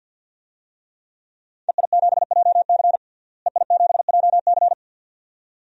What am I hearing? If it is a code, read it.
Morse code, heard as a keyed tone, about 52 wpm, 700 Hz. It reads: EI7JC EI7JC
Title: DXCC-100 Call Signs - 5 Characters - Repeated 2x - 50wpm